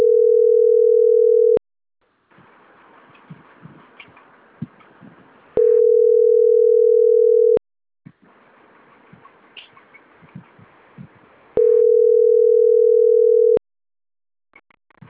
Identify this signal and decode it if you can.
Ringback tone (North America), 440 + 480 Hz.